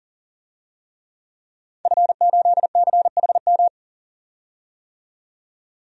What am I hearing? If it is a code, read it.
Morse code, heard as a keyed tone, about 39 wpm, 700 Hz. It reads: F8CHM
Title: DXCC-100 Call Signs - 5 Characters - 40wpm